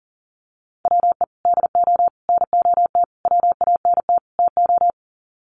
Morse code, heard as a keyed tone, about 40 wpm, 700 Hz. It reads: WE DO NOT WANT TO
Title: Sets of 5 Words 40wpm